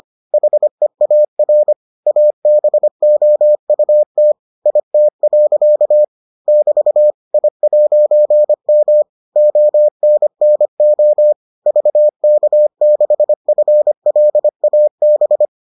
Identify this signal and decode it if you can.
Morse code, heard as a keyed tone, about 25 wpm, 600 Hz. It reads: HEAR ABOUT IT. <BT> I'M ONNO VK6FLAB